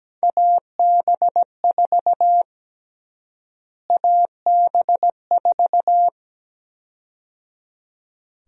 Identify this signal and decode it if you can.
Morse code, heard as a keyed tone, about 17 wpm, 700 Hz. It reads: AB4 AB4
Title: USA Call Sign Prefixes - Repeated 2x - 17wpm